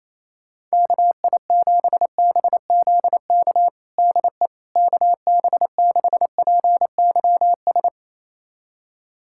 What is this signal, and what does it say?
Morse code, heard as a keyed tone, about 28 wpm, 700 Hz. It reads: KI7BZK DE KB6PYS
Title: USA Callsign DE Callsign - 6 Characters - 28wpm